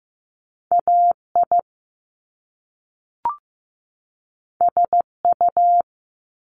Morse code, heard as a keyed tone, about 15 wpm, 700 Hz. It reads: AI SU
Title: Top Two Letter Combinations in Words 15wpm